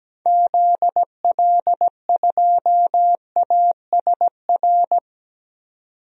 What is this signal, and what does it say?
Morse code, heard as a keyed tone, about 17 wpm, 700 Hz. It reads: ZL2ASR
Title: DXCC-100 Call Signs - 6 Characters - 17wpm